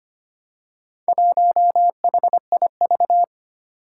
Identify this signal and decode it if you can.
Morse code, heard as a keyed tone, about 25 wpm, 700 Hz. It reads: 1HIV
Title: Four Challenging Letter-Number Combinations 25wpm